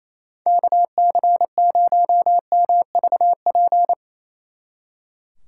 Morse code, heard as a keyed tone, about 35 wpm, 700 Hz. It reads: KC0MVP
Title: Speed Racing - USA Call Signs 28wpm